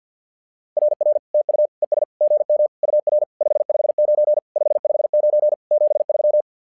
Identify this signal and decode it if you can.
Morse code, heard as a keyed tone, about 48 wpm, 600 Hz. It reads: RR TU ES GM UR 559 559 73
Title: SOTA QSO Practice 50wpm